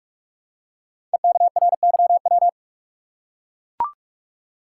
Morse code, heard as a keyed tone, about 44 wpm, 700 Hz. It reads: EKRYW
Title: Code Groups - 5 Letters - 45wpm